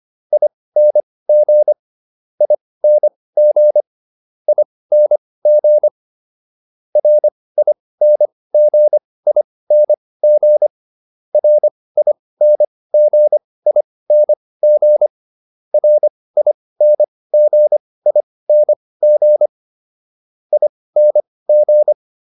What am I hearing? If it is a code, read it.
Morse code, heard as a keyed tone, about 25 wpm, spaced as for 18 wpm, 600 Hz. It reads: ING ING ING RINGING RINGING RINGING ING